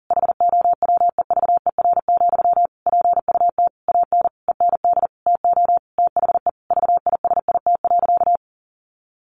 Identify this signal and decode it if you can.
Morse code, heard as a keyed tone, about 40 wpm, 700 Hz. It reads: HOWEVER, PUT AN END TO THE VISIT.
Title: Sentences from Top 1200 Words 40wpm